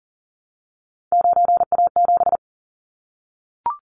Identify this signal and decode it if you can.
Morse code, heard as a keyed tone, about 40 wpm, 700 Hz. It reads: 9A7
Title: Callsign Prefixes for Top 100 DXCC - 40wpm